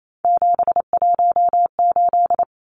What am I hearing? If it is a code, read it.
Morse code, heard as a keyed tone, about 28 wpm, 700 Hz. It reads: 718
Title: Numbers - 3-Digit 28wpm